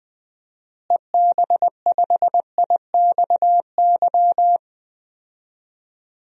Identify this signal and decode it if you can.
Morse code, heard as a keyed tone, about 20 wpm, 700 Hz. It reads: EB5IXY
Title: DXCC-100 Call Signs - 6 Characters - Repeated 3x - 20wpm